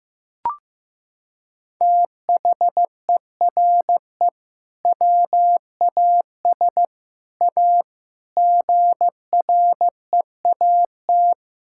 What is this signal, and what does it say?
Morse code, heard as a keyed tone, about 15 wpm, 700 Hz. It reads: THERE WAS A GREAT
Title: Sentences from Top 1600 Words 15wpm